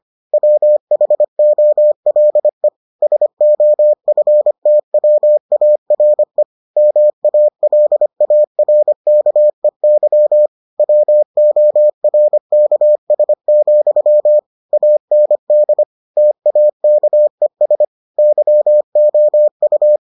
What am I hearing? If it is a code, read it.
Morse code, heard as a keyed tone, about 25 wpm, 600 Hz. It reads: WHOLE SOFTWARE MALARKEY WORKS, AND TAKES YOU